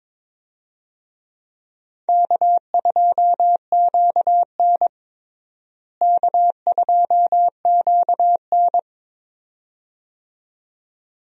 Morse code, heard as a keyed tone, about 22 wpm, 700 Hz. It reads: K2QN K2QN
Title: USA Call Signs - 4 Characters - Repeated 2x - 22wpm